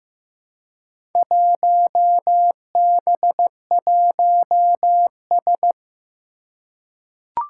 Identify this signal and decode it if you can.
Morse code, heard as a keyed tone, about 15 wpm, 700 Hz. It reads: 1B1S
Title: Four Challenging Letter-Number Combinations 15wpm